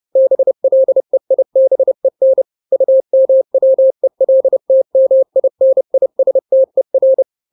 Morse code, heard as a keyed tone, about 30 wpm, 530 Hz. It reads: BLEIBEN UMWELTMINISTER
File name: daily / 29wpm